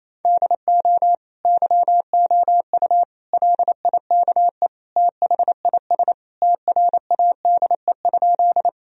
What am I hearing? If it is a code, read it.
Morse code, heard as a keyed tone, about 28 wpm, 700 Hz. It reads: DO YOU LIKE THIS TRADE?
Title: Sentences from Top 1000 Words - Review Part 4 - 28wpm